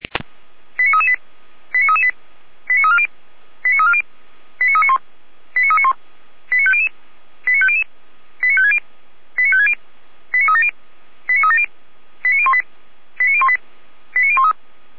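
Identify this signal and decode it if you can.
Five-tone selective call (ZVEI), heard as quick runs of five tones, tones of about 70 ms. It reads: 89338 89338 89340 89340 89391 89391 89500 89500 89560 89560 89370 89370 89018 89018 89013
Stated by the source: Five-tone ZVEI selective call 89310, received by SDR.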